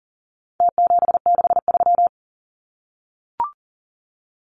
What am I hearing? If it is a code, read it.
Morse code, heard as a keyed tone, about 40 wpm, 700 Hz. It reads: T763
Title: Numbers - Contest Style - 4-Digit 40wpm